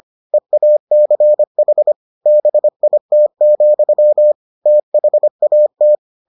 Morse code, heard as a keyed tone, about 25 wpm, 600 Hz. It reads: EACH BIT, THAT